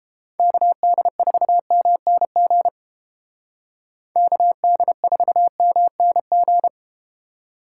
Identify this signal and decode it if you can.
Morse code, heard as a keyed tone, about 31 wpm, 700 Hz. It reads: KD4MNG KD4MNG